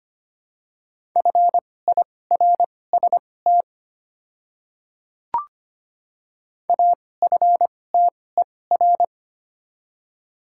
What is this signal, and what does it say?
Morse code, heard as a keyed tone, about 25 wpm, spaced as for 18 wpm, 700 Hz. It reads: FIRST AFTER
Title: 2x Character Spacing - Top 300 Words - 25wpm